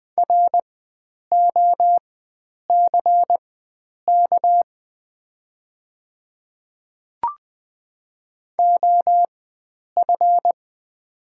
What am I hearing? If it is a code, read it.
Morse code, heard as a keyed tone, about 20 wpm, spaced as for 9 wpm, 690 Hz. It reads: ROCK OF
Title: 4x Character Spacing - Top 500 Words - 20wpm